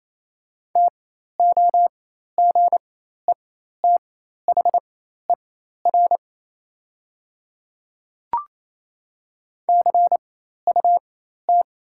Morse code, heard as a keyed tone, about 28 wpm, spaced as for 13 wpm, 690 Hz. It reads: TOGETHER CUT
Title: 4x Character Spacing - Top 300 Words - 28wpm